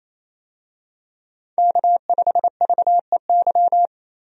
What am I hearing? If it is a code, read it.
Morse code, heard as a keyed tone, about 28 wpm, 700 Hz. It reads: K5VEY
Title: USA Call Signs - 5 Characters - 28wpm